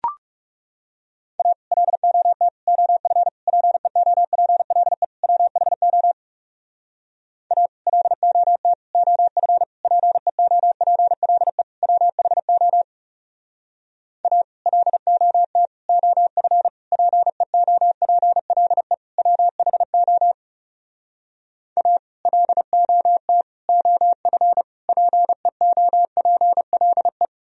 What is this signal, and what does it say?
Morse code, heard as a keyed tone, about 36 wpm, 700 Hz. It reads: A LOT OF PEOPLE WHO A LOT OF PEOPLE WHO A LOT OF PEOPLE WHO A LOT OF PEOPLE
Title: Speed-Racing - Sets of 5 Words 30wpm